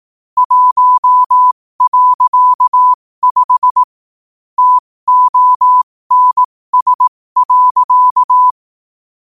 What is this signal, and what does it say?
Morse code, heard as a keyed tone, about 18 wpm, spaced as for 15 wpm, 1000 Hz. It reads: 1.5 TONS.
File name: qotdfiles / 15wpm